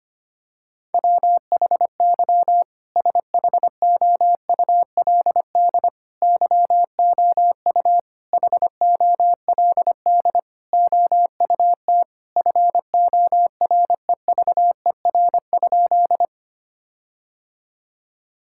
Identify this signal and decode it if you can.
Morse code, heard as a keyed tone, about 25 wpm, 700 Hz. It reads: WHY SHOULD YOU HOLD OUT FOREVER?
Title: Sentences from Top 1600 Words 25wpm